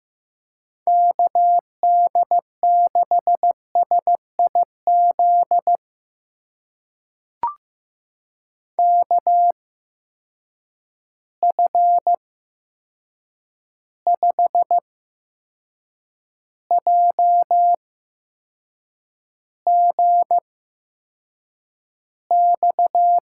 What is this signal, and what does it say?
Morse code, heard as a keyed tone, about 15 wpm, spaced as for 6 wpm, 700 Hz. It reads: KD6SIZ KF5JGX
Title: USA Call Signs - 6 Characters - 8x Character Spacing - 15wpm